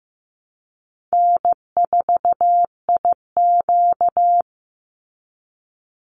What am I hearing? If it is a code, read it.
Morse code, heard as a keyed tone, about 15 wpm, 700 Hz. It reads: N4IQ
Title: Call Signs Repeated 2x - 15wpm